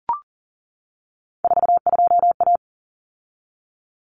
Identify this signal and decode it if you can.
Morse code, heard as a keyed tone, about 40 wpm, 700 Hz. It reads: 42A
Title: Common Cut Numbers - 3-Digit 40wpm